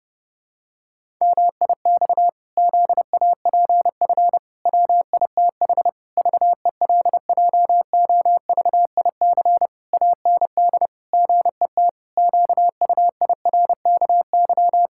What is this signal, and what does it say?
Morse code, heard as a keyed tone, about 30 wpm, 700 Hz. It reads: MIX ZAPF WITH VELJOVIC AND GET QUIRKY
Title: Pangram Challenge v2 - 30wpm